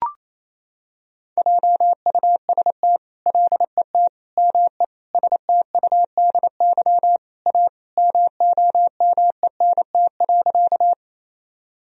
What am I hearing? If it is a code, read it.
Morse code, heard as a keyed tone, about 28 wpm, 700 Hz. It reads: JUST LET ME STUDY A MOMENT.
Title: Sentences from Top 1000 Words - Review Part 1 - 28wpm